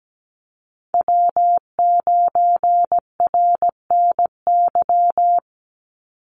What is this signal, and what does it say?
Morse code, heard as a keyed tone, about 17 wpm, 700 Hz. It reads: W9RNY
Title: Call Signs - 17wpm